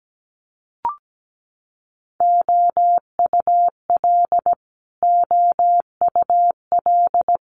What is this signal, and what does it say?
Morse code, heard as a keyed tone, about 17 wpm, 700 Hz. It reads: OUL OUL
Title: Multi-letter Phonemes Repeated 3x 17wpm